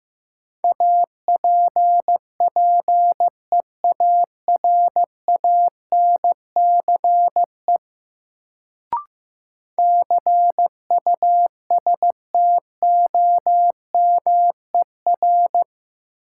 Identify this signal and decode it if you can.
Morse code, heard as a keyed tone, about 16 wpm, 700 Hz. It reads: APPEARANCE CUSTOMER
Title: Sentences from Top 1100 Words - New Word Drill - 15wpm